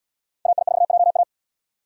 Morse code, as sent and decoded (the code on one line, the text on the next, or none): .- .. ....- .--- .-
AI4JA